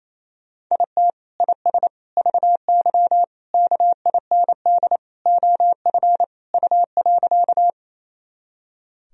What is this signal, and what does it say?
Morse code, heard as a keyed tone, about 28 wpm, 700 Hz. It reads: IT IS VY KIND OF U.
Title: Sentences from Top 200 Words and Abbreviations 28wpm